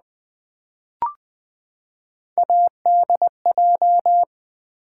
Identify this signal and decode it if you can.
Morse code, heard as a keyed tone, about 20 wpm, 700 Hz. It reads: ADJ
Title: Build A Word - 3 Letters At A Time - 8-Letter Words - 20wpm